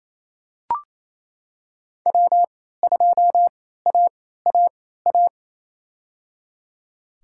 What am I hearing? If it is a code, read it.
Morse code, heard as a keyed tone, about 28 wpm, spaced as for 16 wpm, 700 Hz. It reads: W2AAA
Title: USA Call Signs - 3x Character Spacing - 28wpm